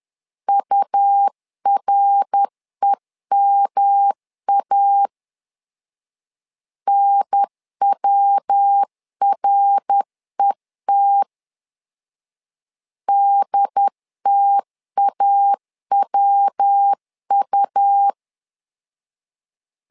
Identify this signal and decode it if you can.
Morse code, heard as a keyed tone, about 10 wpm, 800 Hz. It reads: UREMA NWRET DTAWU
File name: LEZIONI / 3 MWRO / 3 MWR 10wpm 1952